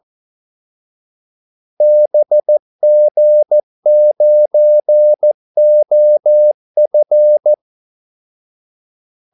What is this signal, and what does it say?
Morse code, heard as a keyed tone, about 14 wpm, 600 Hz. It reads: BG9OF
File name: Callsigns-14wpm